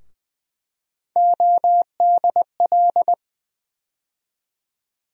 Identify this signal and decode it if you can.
Morse code, heard as a keyed tone, about 20 wpm, 700 Hz. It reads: ODL
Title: USA Call Sign Postfixes - 3 Letters 20wpm